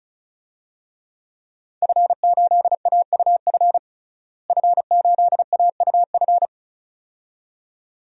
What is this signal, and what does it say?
Morse code, heard as a keyed tone, about 34 wpm, 700 Hz. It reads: F8AUF F8AUF
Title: DXCC-100 Call Signs - 5 Characters - Repeated 2x - 35wpm